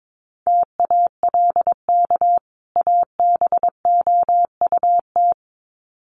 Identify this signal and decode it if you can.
Morse code, heard as a keyed tone, about 22 wpm, 700 Hz. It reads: TALK ABOUT
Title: Sets of 2 Words 22wpm